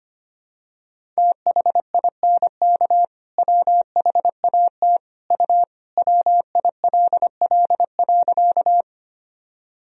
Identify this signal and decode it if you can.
Morse code, heard as a keyed tone, about 25 wpm, 700 Hz. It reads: THINK WHAT U WILL.
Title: Sentences from Top 200 Words and Abbreviations 25wpm